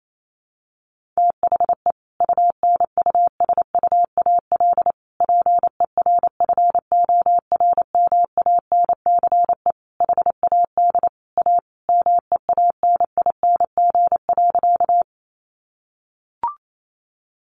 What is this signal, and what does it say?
Morse code, heard as a keyed tone, about 28 wpm, 700 Hz. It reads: THE UNUSUAL PERFORMANCE HAD A MEANING.
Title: Sentences from Top 1400 Words 28wpm